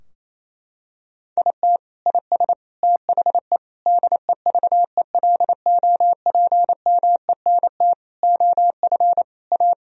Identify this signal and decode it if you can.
Morse code, heard as a keyed tone, about 28 wpm, 700 Hz. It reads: IT IS THE DEVELOPMENT OF A
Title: Sentences from Top 1000 Words - Review Part 4 - 28wpm